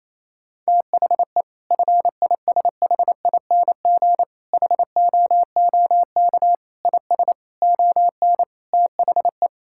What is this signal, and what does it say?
Morse code, heard as a keyed tone, about 28 wpm, 700 Hz. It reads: THE FISHING HOOK IS ON THE
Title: Sentences from Top 1600 Words 28wpm